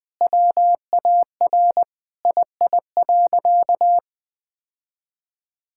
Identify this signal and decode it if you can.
Morse code, heard as a keyed tone, about 21 wpm, 700 Hz. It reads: WAR II.
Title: Sentences with 7-Word Phrases - 20wpm